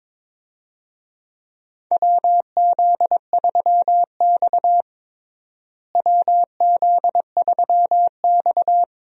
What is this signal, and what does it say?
Morse code, heard as a keyed tone, about 22 wpm, 700 Hz. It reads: WZ3X WZ3X
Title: USA Call Signs - 4 Characters - Repeated 2x - 22wpm